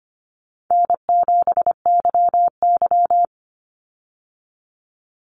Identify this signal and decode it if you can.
Morse code, heard as a keyed tone, about 25 wpm, 700 Hz. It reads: N7YY
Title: Call Signs Repeated 2x - 25wpm